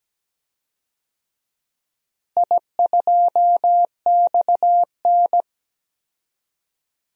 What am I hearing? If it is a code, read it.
Morse code, heard as a keyed tone, about 17 wpm, 700 Hz. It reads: I2XN
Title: DXCC-100 Call Signs - 4 Characters - Repeated 2x - 17wpm